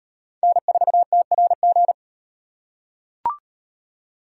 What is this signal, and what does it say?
Morse code, heard as a keyed tone, about 38 wpm, 700 Hz. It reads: N4TRG